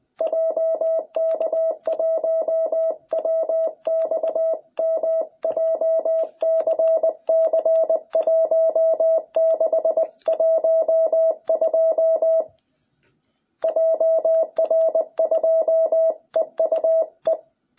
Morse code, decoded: JK1WXM JCC1612 JR2EUE

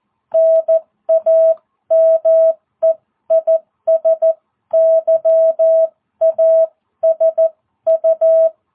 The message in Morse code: NAME IS YASU

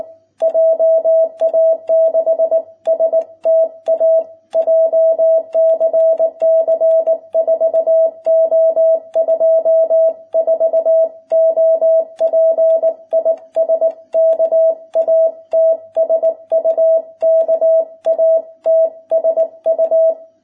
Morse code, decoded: JA6STA JCC 4O24 OP IS KATSU KATSU

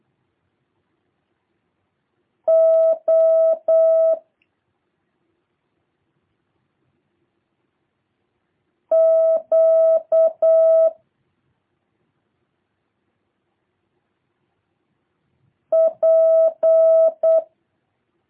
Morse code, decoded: O Q P

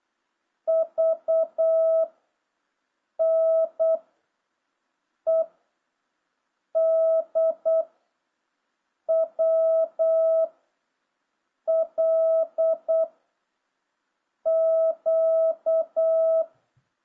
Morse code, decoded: V N E D W L Q

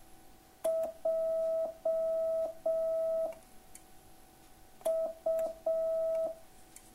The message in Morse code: J U

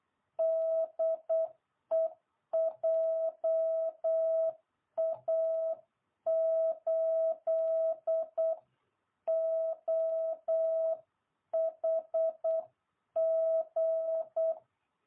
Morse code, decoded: DE JA8OHG

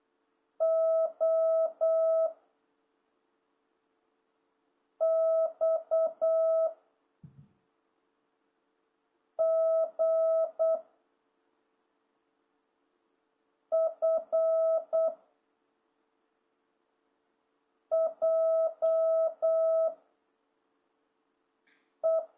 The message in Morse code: O X G F J E

